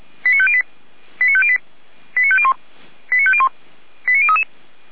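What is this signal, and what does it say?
Five-tone selective call (ZVEI), heard as quick runs of five tones, tones of about 70 ms. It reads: 89508 89508 89501 89501 89033
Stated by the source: ZVEI five-tone selective call 89001 received by SDR, from fire brigade Orlamünde (feuerwehr-orlamuende) server.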